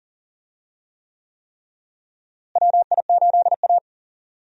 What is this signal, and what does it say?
Morse code, heard as a keyed tone, about 38 wpm, 700 Hz. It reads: WI8A
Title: USA Call Signs - 4 Characters - Repeated 3x - 40wpm